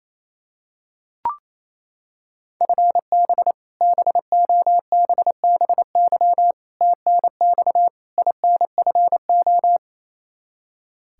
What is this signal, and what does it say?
Morse code, heard as a keyed tone, about 28 wpm, 700 Hz. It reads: FB BOBBY TNX INFO
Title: Practice QSO - Exchange 3 - More Info - 28wpm